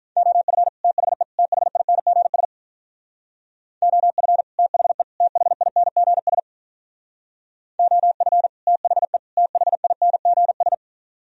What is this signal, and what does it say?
Morse code, heard as a keyed tone, about 46 wpm, 700 Hz. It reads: OF THE THINGS OF THE THINGS OF THE THINGS